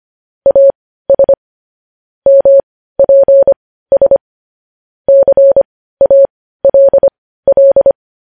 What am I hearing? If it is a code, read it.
Morse code, heard as a keyed tone, about 25 wpm, spaced as for 15 wpm, 550 Hz. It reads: AS MPS CALL
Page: news-headlines-in-morse-code-at-15-wpm